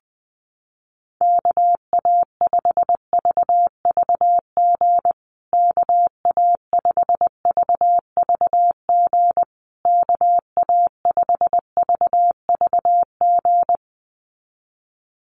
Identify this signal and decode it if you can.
Morse code, heard as a keyed tone, about 20 wpm, 700 Hz. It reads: KA5VVG KA5VVG KA5VVG
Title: USA Call Signs Repeated 3x 20wpm